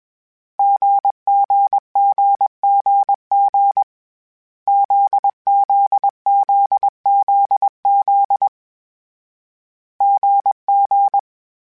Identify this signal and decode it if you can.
Morse code, heard as a keyed tone, about 21 wpm, 800 Hz. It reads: GGGGG ZZZZZ GG
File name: CW Code Groups-mp3s / Sets mp3s / GZ GZ7 sp @20